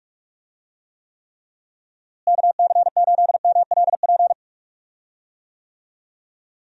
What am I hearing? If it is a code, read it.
Morse code, heard as a keyed tone, about 46 wpm, 700 Hz. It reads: KC8MLP
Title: USA Call Signs - 6 Characters - Repeated 3x - 45wpm